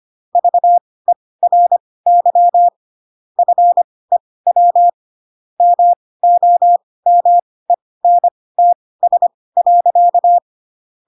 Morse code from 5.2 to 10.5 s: MOMENTS.